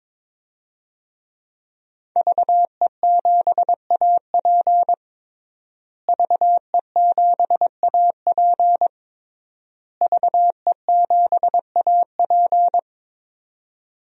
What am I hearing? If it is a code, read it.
Morse code, heard as a keyed tone, about 22 wpm, 700 Hz. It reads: VE7AP VE7AP VE7AP
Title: DXCC-100 Call Signs - 5 Characters - Repeated 3x - 22wpm